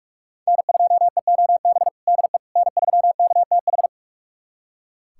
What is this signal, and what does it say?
Morse code, heard as a keyed tone, about 46 wpm, 700 Hz. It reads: N2EOB DE N3KTH
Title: USA Callsign DE Callsign - 5 Characters - 45wpm